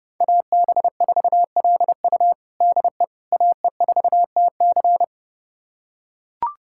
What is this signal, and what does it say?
Morse code, heard as a keyed tone, about 30 wpm, 700 Hz. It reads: AB4LU DE AE4TC
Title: USA Callsign DE Callsign - 5 Characters - 30wpm